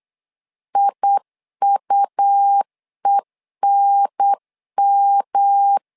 Morse code, heard as a keyed tone, about 8 wpm, 800 Hz. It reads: IUENM
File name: LEZIONI / 3 MWRO / 3 M 8wpm 2300